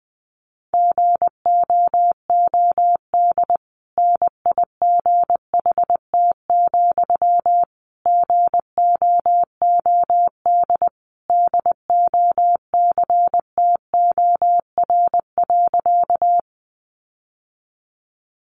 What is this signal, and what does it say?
Morse code, heard as a keyed tone, about 20 wpm, 700 Hz. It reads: GOOD NIGHT, GOOD DOCTOR.
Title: Sentences from Top 1500 Words 20wpm